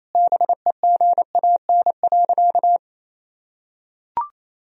Morse code, heard as a keyed tone, about 28 wpm, 700 Hz. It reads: BEGAN.
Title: Sentences from Top 1000 Words - Review Part 1 - 28wpm